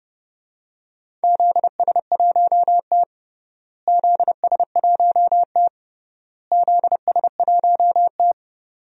Morse code, heard as a keyed tone, about 30 wpm, 700 Hz. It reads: ZS1T ZS1T ZS1T
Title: DXCC-100 Call Signs - 4 Characters - Repeated 3x - 30wpm